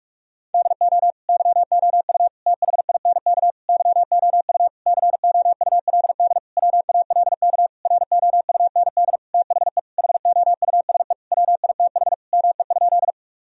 Morse code, heard as a keyed tone, about 41 wpm, 700 Hz. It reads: DO YOU THINK YOU COULD WALK ROUND THE HOUSE WITH ME?
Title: Sentences from Top 1000 Words - Review Part 1 - 45wpm